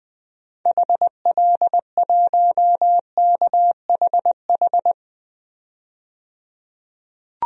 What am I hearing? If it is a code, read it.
Morse code, heard as a keyed tone, about 20 wpm, 700 Hz. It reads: HL1KHH
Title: DXCC-100 Call Signs - 6 Characters - 20wpm